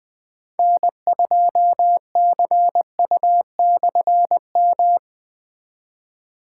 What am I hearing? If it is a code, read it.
Morse code, heard as a keyed tone, about 20 wpm, 700 Hz. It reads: N2CU/M
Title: K1USN SST QSO Practice - 20wpm